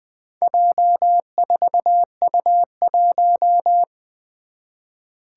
Morse code, heard as a keyed tone, about 20 wpm, 700 Hz. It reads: J4U1
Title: Four Challenging Letter-Number Combinations 20wpm